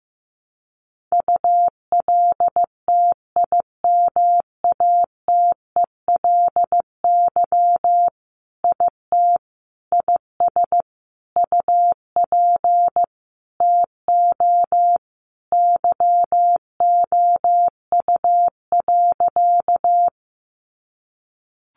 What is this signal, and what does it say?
Morse code, heard as a keyed tone, about 15 wpm, 700 Hz. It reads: ULTIMATELY IT IS UP TO YOU.
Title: Sentences from Top 1200 Words 15wpm